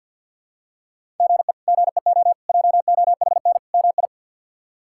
Morse code, heard as a keyed tone, about 49 wpm, 690 Hz. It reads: GE GEO JOHN MI
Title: K1USN SST QSO Practice - 50wpm